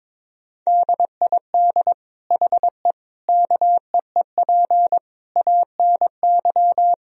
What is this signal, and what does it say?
Morse code, heard as a keyed tone, about 22 wpm, 700 Hz. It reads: DID HE KEEP ANY
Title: Sentences from Top 1000 Words - Review Part 3 - 22wpm